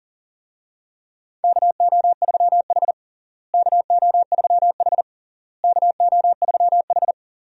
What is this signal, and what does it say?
Morse code, heard as a keyed tone, about 40 wpm, 700 Hz. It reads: KO3H KO3H KO3H
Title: USA Call Signs - 4 Characters - Repeated 3x - 40wpm